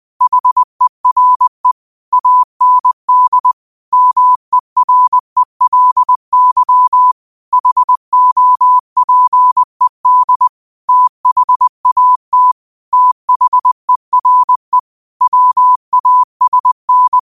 Morse code, heard as a keyed tone, about 20 wpm, 1000 Hz. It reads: HERE AND MERELY HOPED THAT THERE WASN
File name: qotdfiles / 20wpm